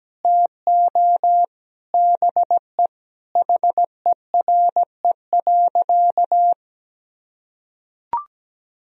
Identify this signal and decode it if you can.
Morse code, heard as a keyed tone, about 17 wpm, 700 Hz. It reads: TO BE HERE.
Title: Sentences from Top 1000 Words - Review Part 1 - 17wpm